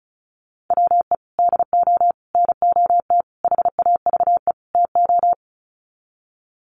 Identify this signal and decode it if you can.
Morse code, heard as a keyed tone, about 34 wpm, 700 Hz. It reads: WE DO NOT HAVE TO
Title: Sets of 5 Words 35wpm